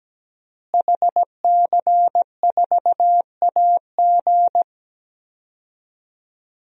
Morse code, heard as a keyed tone, about 17 wpm, 700 Hz. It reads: HC4AG
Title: DXCC-100 Call Signs - 5 Characters - Repeated 2x - 17wpm